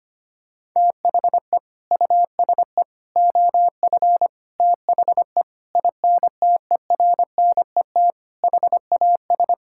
Morse code, heard as a keyed tone, about 25 wpm, 700 Hz. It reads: THE USE OF THE INTERNET HAS
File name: practice / 25wpm / Sentences with 4-Word Phrases - 25wpm